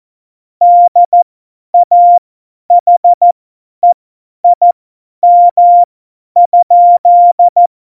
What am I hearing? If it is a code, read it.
Morse code, heard as a keyed tone, about 14 wpm, spaced as for 10 wpm, 700 Hz. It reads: DAHEIM?
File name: Morsekurs / Witze / Witz 03. November 2020, 50 F70